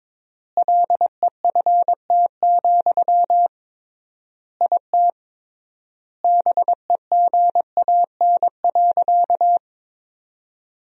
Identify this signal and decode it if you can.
Morse code, heard as a keyed tone, about 22 wpm, 700 Hz. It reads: LEFT, IT BEGAN.
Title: 3x Word-Spacing - Sentences from Top 400 Words - 22wpm